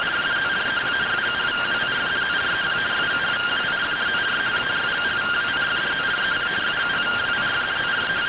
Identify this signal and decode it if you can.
Two-tone teleprinter signal, about 170 Hz shift.